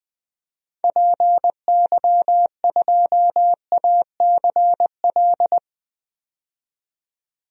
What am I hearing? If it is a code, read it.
Morse code, heard as a keyed tone, about 20 wpm, 700 Hz. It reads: PY2ACL
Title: DXCC-100 Call Signs - 6 Characters - 20wpm